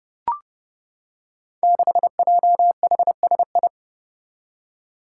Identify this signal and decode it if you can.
Morse code, heard as a keyed tone, about 31 wpm, 700 Hz. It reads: 6JHSI
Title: Five Challenging Letter-Number Combinations 30wpm